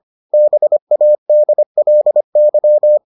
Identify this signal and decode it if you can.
Morse code, heard as a keyed tone, about 25 wpm, 600 Hz. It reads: BADLY